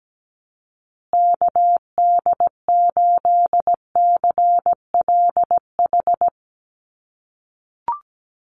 Morse code, heard as a keyed tone, about 17 wpm, 700 Hz. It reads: KD8CLH
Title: USA Call Signs Repeated 2x 17wpm